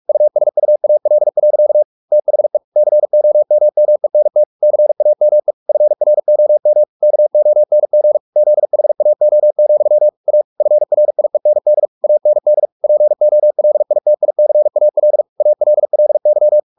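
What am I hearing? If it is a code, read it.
Morse code, heard as a keyed tone, about 44 wpm, 600 Hz. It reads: USUAL. THE COMMENT CAME FROM KONG ZHAO, A FRIEND AND POLITICAL ALLY
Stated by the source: Morse practice at 45 wpm, the novel Never by Ken Follett in English, from CW med Gnister.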